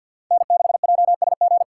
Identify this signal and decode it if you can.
Morse code, heard as a keyed tone, about 52 wpm, 700 Hz. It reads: N6JSG
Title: USA Call Signs - 4x Character Spacing - 50wpm